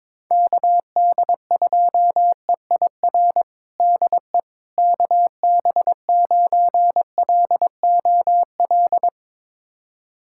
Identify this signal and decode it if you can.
Morse code, heard as a keyed tone, about 22 wpm, 700 Hz. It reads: KD2EIR DE KB9LOL
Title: USA Callsign DE Callsign - 6 Characters - 22wpm